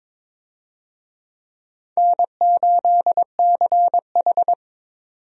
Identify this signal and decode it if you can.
Morse code, heard as a keyed tone, about 23 wpm, 700 Hz. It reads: N8CH
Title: USA Call Signs - 4 Characters - 22wpm